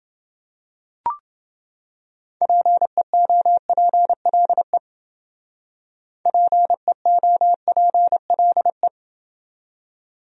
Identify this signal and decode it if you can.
Morse code, heard as a keyed tone, about 28 wpm, 700 Hz. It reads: PEOPLE PEOPLE